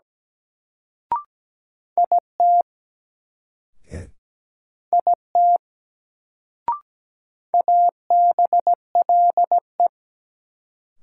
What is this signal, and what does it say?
Morse code, heard as a keyed tone, about 17 wpm, 700 Hz. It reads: IT IT ABLE
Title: Sets of 2 Words - New Word Drill - 17wpm